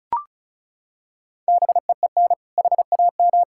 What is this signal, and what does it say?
Morse code, heard as a keyed tone, about 36 wpm, 700 Hz. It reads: BEEN HAM
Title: Practice QSO - Exchange 3 - More Info - 35wpm